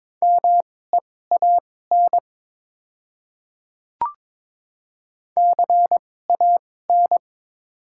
Morse code, heard as a keyed tone, about 21 wpm, spaced as for 16 wpm, 700 Hz. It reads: MEAN CAN
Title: 2x Character Spacing - Top 200 Words - 22wpm